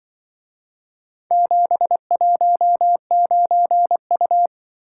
Morse code, heard as a keyed tone, about 24 wpm, 700 Hz. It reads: 719U
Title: Speed-Racing - Four Letter-Number Combinations - 20wpm